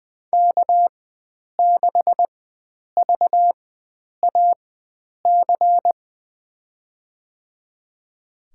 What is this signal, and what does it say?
Morse code, heard as a keyed tone, about 20 wpm, spaced as for 9 wpm, 700 Hz. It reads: K6VAC
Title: USA Call Signs - 4x Character Spacing - 20wpm